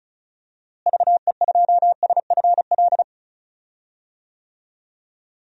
Morse code, heard as a keyed tone, about 34 wpm, 700 Hz. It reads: VE2SFL
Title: DXCC-100 Call Signs - 6 Characters - Repeated 3x - 35wpm